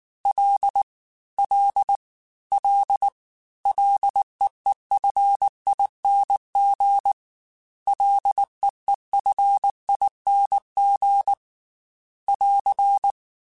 Morse code, heard as a keyed tone, about 19 wpm, 800 Hz. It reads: L L L LEEFING LEEFING <AR>